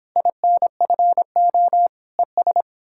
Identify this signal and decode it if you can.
Morse code, heard as a keyed tone, about 29 wpm, 700 Hz. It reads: INFO ES